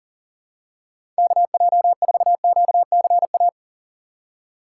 Morse code, heard as a keyed tone, about 37 wpm, 700 Hz. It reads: KJ4QCA
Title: USA Call Signs - 6 Characters - 4x Character Spacing - 40wpm